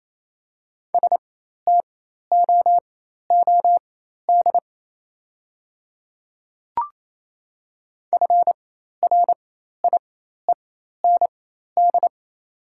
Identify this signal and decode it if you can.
Morse code, heard as a keyed tone, about 28 wpm, spaced as for 13 wpm, 700 Hz. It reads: STOOD FRIEND
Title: 4x Character Spacing - Top 500 Words - 28wpm